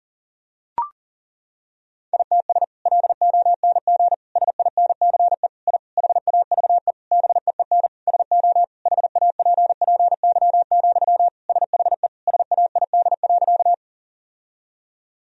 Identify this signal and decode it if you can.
Morse code, heard as a keyed tone, about 40 wpm, 700 Hz. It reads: ITS LONG SINCE I HAVE BEEN SO HAPPY, SHE SAID.
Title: Sentences from Top 1000 Words - Review Part 1 - 40wpm